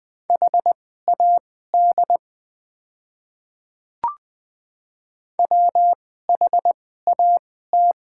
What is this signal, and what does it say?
Morse code, heard as a keyed tone, about 20 wpm, spaced as for 14 wpm, 700 Hz. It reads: HAD WHAT
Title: 2x Character Spacing - Top 100 Words - 20wpm